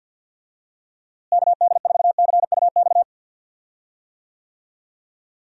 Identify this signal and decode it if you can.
Morse code, heard as a keyed tone, about 52 wpm, 700 Hz. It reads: KD4CUX
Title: USA Call Signs - 6 Characters - Repeated 2x - 50wpm